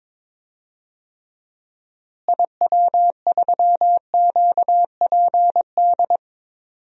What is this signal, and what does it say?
Morse code, heard as a keyed tone, about 21 wpm, 700 Hz. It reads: IW3QPD
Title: DXCC-100 Call Signs - 6 Characters - Repeated 3x - 22wpm